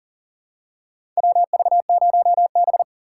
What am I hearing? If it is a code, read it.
Morse code, heard as a keyed tone, about 40 wpm, 690 Hz. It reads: WV0B
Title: USA Call Signs - 4 Characters - 2x Character Spacing - 40wpm